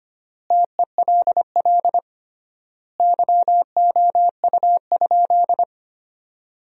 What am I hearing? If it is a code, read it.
Morse code, heard as a keyed tone, about 25 wpm, 700 Hz. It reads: TELL YOU?
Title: 3x Word-Spacing - Sentences from Top 200 Words - 25wpm